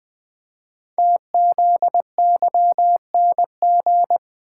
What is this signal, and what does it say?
Morse code, heard as a keyed tone, about 20 wpm, 700 Hz. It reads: TZYNG
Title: Code Groups - 5 Letter:Numbers - 20wpm